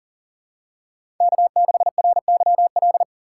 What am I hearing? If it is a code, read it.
Morse code, heard as a keyed tone, about 40 wpm, 700 Hz. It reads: K6RYL
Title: USA Call Signs - 5 Characters - 2x Character Spacing - 40wpm